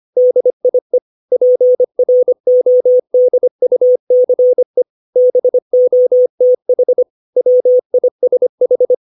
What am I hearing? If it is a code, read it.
Morse code, heard as a keyed tone, about 25 wpm, 500 Hz. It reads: DIE PRODUCE BOTH WISH
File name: 632 words / 632 Words 25WPM